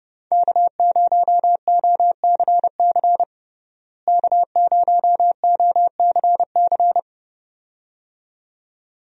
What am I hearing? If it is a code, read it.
Morse code, heard as a keyed tone, about 30 wpm, 700 Hz. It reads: K0OCC K0OCC
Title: USA Call Signs - 5 Characters - Repeated 2x - 30wpm